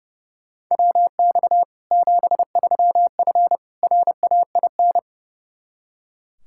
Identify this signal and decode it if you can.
Morse code, heard as a keyed tone, about 30 wpm, 700 Hz. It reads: WX 73F RAIN
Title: Practice QSO - Exchange 2 - Info - 30wpm